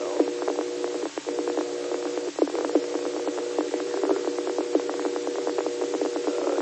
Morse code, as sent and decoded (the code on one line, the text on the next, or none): none